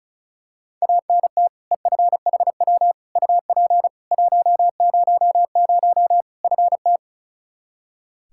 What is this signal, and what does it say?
Morse code, heard as a keyed tone, about 36 wpm, 700 Hz. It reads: ANT EFHW UP 100 FT
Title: Practice QSO - Exchange 2 - Info - 35wpm